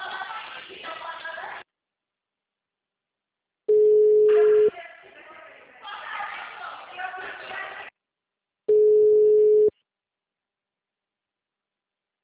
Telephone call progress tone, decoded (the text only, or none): ringback tone